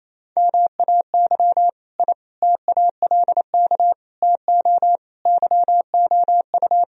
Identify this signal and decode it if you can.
Morse code, heard as a keyed tone, about 28 wpm, 700 Hz. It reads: MAY I TALK TO YOU
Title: Sentences from Top 1000 Words - Review Part 4 - 28wpm